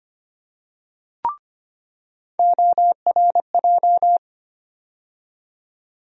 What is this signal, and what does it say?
Morse code, heard as a keyed tone, about 25 wpm, 700 Hz. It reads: ORJ
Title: USA Call Sign Postfixes - 3 Letters 25wpm